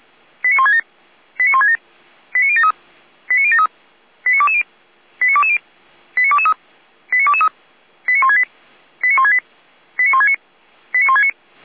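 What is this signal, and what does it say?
Five-tone selective call (ZVEI), heard as quick runs of five tones, tones of about 80 ms. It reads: 89167 89167 89073 89073 89220 89220 89223 89223 89168 89168 89169 89169
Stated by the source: ZVEI five-tone selective call received on SDR.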